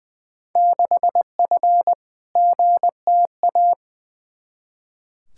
Morse code, heard as a keyed tone, about 20 wpm, 700 Hz. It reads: 6F GTA
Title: Field Day Exchanges Encore - 20wpm